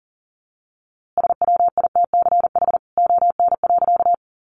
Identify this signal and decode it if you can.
Morse code, heard as a keyed tone, about 40 wpm, 700 Hz. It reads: SWITCH ON.
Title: A Sound Of Thunder 40wpm